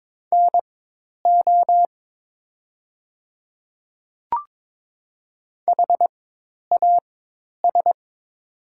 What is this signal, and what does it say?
Morse code, heard as a keyed tone, about 22 wpm, spaced as for 10 wpm, 700 Hz. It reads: NO HAS
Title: 4x Character Spacing - Top 100 Words - 22wpm